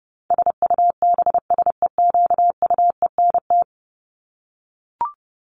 Morse code, heard as a keyed tone, about 30 wpm, 700 Hz. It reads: SUBSEQUENT
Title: LY Suffix - Top 100 Words with Base Words 30wpm